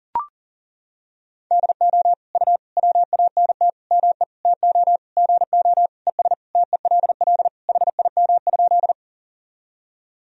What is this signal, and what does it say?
Morse code, heard as a keyed tone, about 40 wpm, 700 Hz. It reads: DO U WANT ME TO GO ES TELL HIM?
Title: Sentences from Top 200 Words and Abbreviations 40wpm